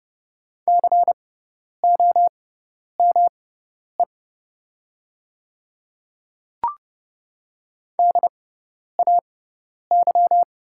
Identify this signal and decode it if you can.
Morse code, heard as a keyed tone, about 30 wpm, spaced as for 10 wpm, 690 Hz. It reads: COME DAY
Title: 6x Character Spacing - Top 300 Words - 30wpm